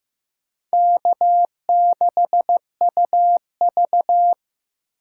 Morse code, heard as a keyed tone, about 18 wpm, 700 Hz. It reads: K6UV
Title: Speed Racing - USA 1x2 Call Signs 15wpm